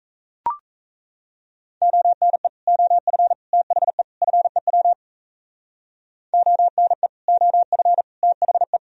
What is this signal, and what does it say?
Morse code, heard as a keyed tone, about 34 wpm, 700 Hz. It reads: ONE OF THE FEW ONE OF THE
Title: Speed-Racing - Sets of 4 Words 28wpm